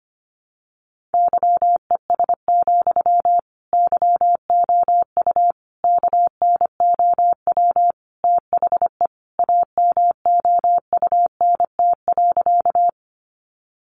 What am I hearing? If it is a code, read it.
Morse code, heard as a keyed tone, about 25 wpm, 700 Hz. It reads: YES, YOU KNOW THE AMOUNT.
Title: Sentences from Top 700 Words 25wpm